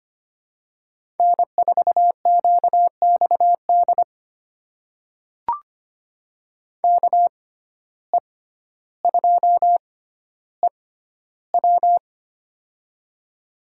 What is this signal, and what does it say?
Morse code, heard as a keyed tone, about 25 wpm, spaced as for 13 wpm, 700 Hz. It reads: N4QXD KE2EW
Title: USA Call Signs - 5 Characters - 6x Character Spacing - 25wpm